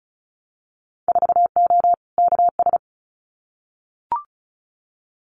Morse code, heard as a keyed tone, about 36 wpm, 700 Hz. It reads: 4O KS
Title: Winter Field Day Exchanges Encore 35wpm